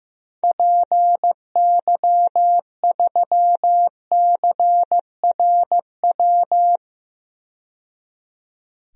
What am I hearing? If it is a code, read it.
Morse code, heard as a keyed tone, about 15 wpm, 700 Hz. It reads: PY3CRW
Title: DXCC-100 Call Signs - 6 Characters - 15wpm